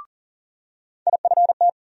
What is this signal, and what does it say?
Morse code, heard as a keyed tone, about 40 wpm, 700 Hz. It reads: IFT
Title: USA Call Sign Postfixes - 3 Letters 40wpm